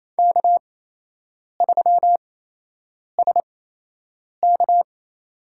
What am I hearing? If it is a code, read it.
Morse code, heard as a keyed tone, about 28 wpm, spaced as for 8 wpm, 700 Hz. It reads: K3SK
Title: USA Call Signs - 4 Characters - 8x Character Spacing - 28wpm